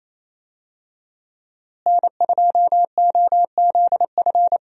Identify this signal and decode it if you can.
Morse code, heard as a keyed tone, about 28 wpm, 700 Hz. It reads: N2OZF
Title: USA Call Signs - 5 Characters - Repeated 2x - 28wpm